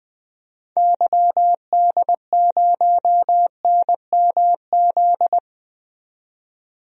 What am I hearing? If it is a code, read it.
Morse code, heard as a keyed tone, about 20 wpm, 700 Hz. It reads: YD0NMZ
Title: DXCC-100 Call Signs - 6 Characters - 20wpm